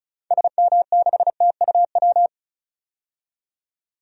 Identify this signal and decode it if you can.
Morse code, heard as a keyed tone, about 34 wpm, 700 Hz. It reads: SM6TUW
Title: DXCC-100 Call Signs - 6 Characters - 35wpm